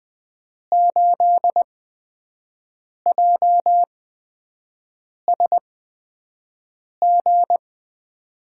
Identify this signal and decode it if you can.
Morse code, heard as a keyed tone, about 20 wpm, 700 Hz. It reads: 8 J S G
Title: USA Call Signs - 5 Characters - 8x Character Spacing - 20wpm